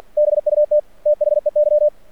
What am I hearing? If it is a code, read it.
Morse code, heard as a keyed tone, about 51 wpm, 600 Hz. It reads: BUT THEY
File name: archived / CRSnet